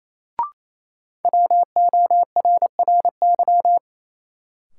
Morse code, heard as a keyed tone, about 28 wpm, 700 Hz. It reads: WORRY
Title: Sets of 5 Words - New Word Drill - 28wpm